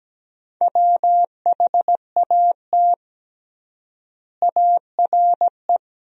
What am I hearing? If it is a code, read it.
Morse code, heard as a keyed tone, about 17 wpm, 700 Hz. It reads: WHAT ARE
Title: 3x Word-Spacing - Sentences from Top 100 Words - 17wpm